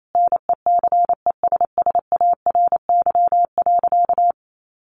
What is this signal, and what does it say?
Morse code, heard as a keyed tone, about 28 wpm, 700 Hz. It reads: NECESSARY.
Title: Sentences from Top 1400 Words 28wpm